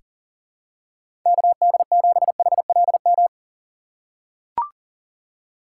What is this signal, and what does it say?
Morse code, heard as a keyed tone, about 40 wpm, 700 Hz. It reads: KD7HLM
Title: USA Call Signs - 6 Characters - 3x Character Spacing - 40wpm